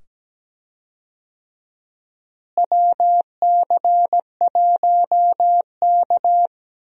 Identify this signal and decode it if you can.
Morse code, heard as a keyed tone, about 17 wpm, 700 Hz. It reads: WC1K
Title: USA Call Signs - 4 Characters - Repeated 2x - 17wpm